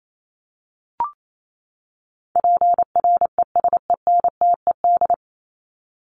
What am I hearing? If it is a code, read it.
Morse code, heard as a keyed tone, about 28 wpm, 700 Hz. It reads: PRESENTED
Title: ED Suffix - Top 100 Words with Base Words 28wpm